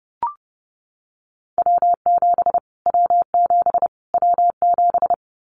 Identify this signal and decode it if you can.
Morse code, heard as a keyed tone, about 30 wpm, 700 Hz. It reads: W7 W7 W7
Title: Callsign Prefixes for Top 25 DXCC - Repeated 3x - 30wpm